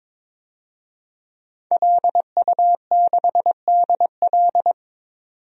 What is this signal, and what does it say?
Morse code, heard as a keyed tone, about 22 wpm, 700 Hz. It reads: LU6DL
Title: DXCC-100 Call Signs - 5 Characters - 22wpm